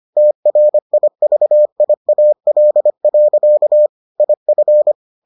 Morse code, heard as a keyed tone, about 25 wpm, 600 Hz. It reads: TRIVIAL. IF